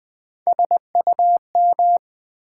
Morse code, heard as a keyed tone, about 20 wpm, 700 Hz. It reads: SUM
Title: Build A Word - 1 Letter At A Time - 3-Letter Words - 20wpm